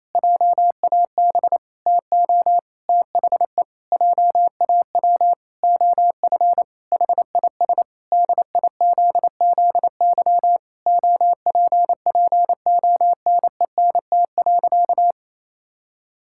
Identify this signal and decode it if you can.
Morse code, heard as a keyed tone, about 28 wpm, 700 Hz. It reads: JAB TO THE JAW OF HIS DIZZY OPPONENT.
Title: Pangram Challenge v2 - 28wpm